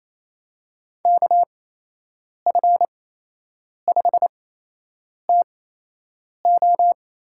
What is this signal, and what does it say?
Morse code, heard as a keyed tone, about 28 wpm, spaced as for 8 wpm, 700 Hz. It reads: KF5TO
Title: USA Call Signs - 8x Character Spacing - 28wpm